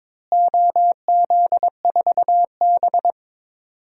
Morse code, heard as a keyed tone, about 21 wpm, 700 Hz. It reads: OZ4B
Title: DXCC-100 Call Signs - 4 Characters - Repeated 2x - 22wpm